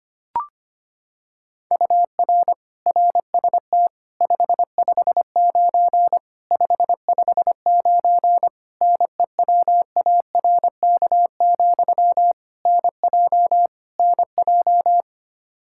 Morse code, heard as a keyed tone, about 25 wpm, 700 Hz. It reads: UR RST 559 559 NEWARK, NJ NJ
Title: Practice QSO - Exchange 1 - Beginning - 25wpm